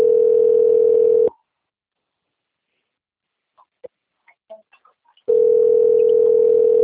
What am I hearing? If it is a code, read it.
Ringback tone (North America), 440 + 480 Hz.
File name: out- - - - - .14307